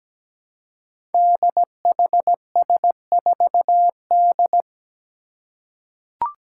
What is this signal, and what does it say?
Morse code, heard as a keyed tone, about 17 wpm, 700 Hz. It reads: DHS4D
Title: Five Challenging Letter-Number Combinations 17wpm